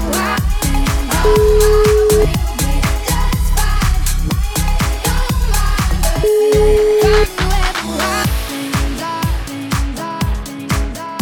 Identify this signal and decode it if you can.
Ringback tone (Europe, 425 Hz), with noise.